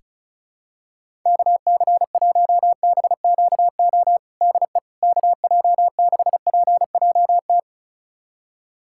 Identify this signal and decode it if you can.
Morse code, heard as a keyed tone, about 36 wpm, 700 Hz. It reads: KC1BQO DE KJ6PJT
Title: USA Callsign DE Callsign - 6 Characters - 35wpm